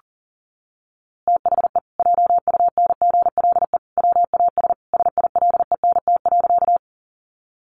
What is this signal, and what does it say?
Morse code, heard as a keyed tone, about 40 wpm, 700 Hz. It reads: THE JUNGLE WAS SILENT.
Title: A Sound Of Thunder 40wpm